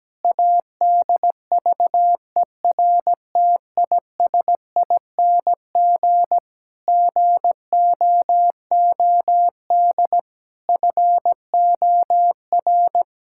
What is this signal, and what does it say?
Morse code, heard as a keyed tone, about 17 wpm, 700 Hz. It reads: ADVERTISING GOOD FOR
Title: Sentences from Top 1000 Words - Review Part 6 - 17wpm